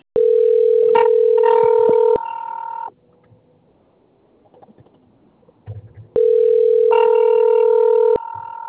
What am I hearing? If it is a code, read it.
Ringback tone (North America), 440 + 480 Hz.